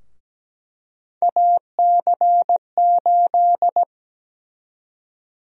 Morse code, heard as a keyed tone, about 17 wpm, 700 Hz. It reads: AC8
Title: USA Call Sign Prefixes - 17wpm